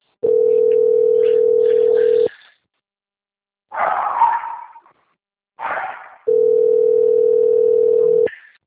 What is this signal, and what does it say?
Ringback tone (North America), 440 + 480 Hz.
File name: out- - - - - .23841